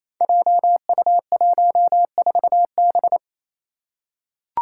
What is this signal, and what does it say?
Morse code, heard as a keyed tone, about 28 wpm, 700 Hz. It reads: JU14B
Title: Five Challenging Letter-Number Combinations 28wpm